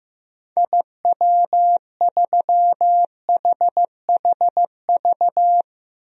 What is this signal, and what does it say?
Morse code, heard as a keyed tone, about 15 wpm, 700 Hz. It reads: IW3HHV
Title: DXCC-100 Call Signs - 6 Characters - 15wpm